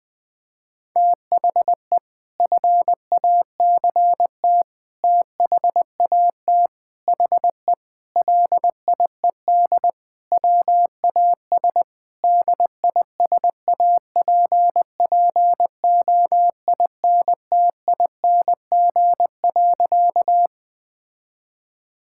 Morse code, heard as a keyed tone, about 21 wpm, 700 Hz. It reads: THE FACT THAT HE LIED WAS DISAPPOINTING.
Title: Sentences with 4-Word Phrases - 20wpm